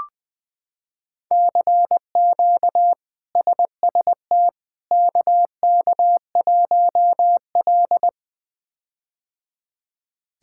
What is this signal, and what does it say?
Morse code, heard as a keyed tone, about 21 wpm, 700 Hz. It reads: CQ SST KK1L
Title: K1USN SST QSO Practice - 20wpm